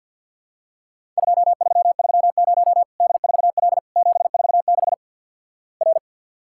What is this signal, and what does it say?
Morse code, heard as a keyed tone, about 54 wpm, 700 Hz. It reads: 2330 D4L 746 R
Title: Worked All Europe DX Contest - QTC Exchange with Repeats - 50wpm